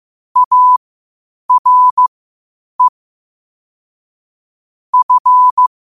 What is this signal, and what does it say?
Morse code, heard as a keyed tone, about 15 wpm, spaced as for 8 wpm, 1000 Hz. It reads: ARE F
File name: qotdfiles / 8wpm